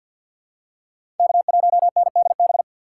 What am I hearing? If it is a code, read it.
Morse code, heard as a keyed tone, about 51 wpm, 700 Hz. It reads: K1NDB
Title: USA Call Signs - 3x Character Spacing - 50wpm